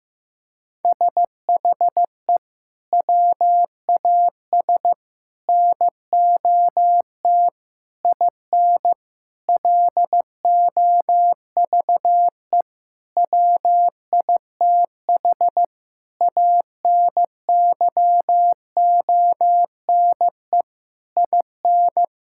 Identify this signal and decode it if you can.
Morse code, heard as a keyed tone, about 15 wpm, 700 Hz. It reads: SHE WAS NOT IN LOVE WITH ANYONE IN
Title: Sentences from Top 1000 Words - Review Part 5 - 15wpm